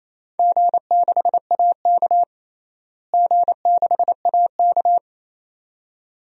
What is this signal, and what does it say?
Morse code, heard as a keyed tone, about 27 wpm, 700 Hz. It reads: G6AK G6AK
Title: DXCC-100 Call Signs - 4 Characters - Repeated 2x - 28wpm